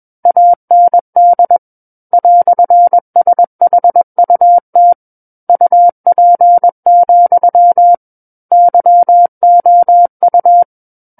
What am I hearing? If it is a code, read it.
Morse code, heard as a keyed tone, about 21 wpm, 700 Hz. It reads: AND "SHUT UP, YOU
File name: Book One / 12-Chapter 12 - 21 WPM